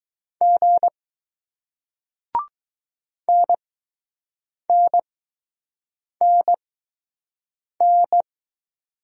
Morse code, heard as a keyed tone, about 20 wpm, 700 Hz. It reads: G N N N N